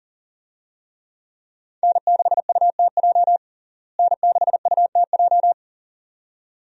Morse code, heard as a keyed tone, about 40 wpm, 700 Hz. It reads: N6UTJ N6UTJ
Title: USA Call Signs - 5 Characters - Repeated 2x - 40wpm